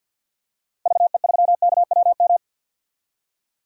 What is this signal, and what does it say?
Morse code, heard as a keyed tone, about 47 wpm, 700 Hz. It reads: VE3KWM
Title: DXCC-100 Call Signs - 6 Characters - Repeated 3x - 50wpm